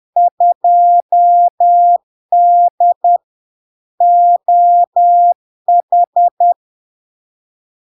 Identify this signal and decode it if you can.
Morse code, heard as a keyed tone, about 10 wpm, 700 Hz. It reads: 2D OH
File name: LARK-FD-2022-Log-in-CW-10wpm-700-Hz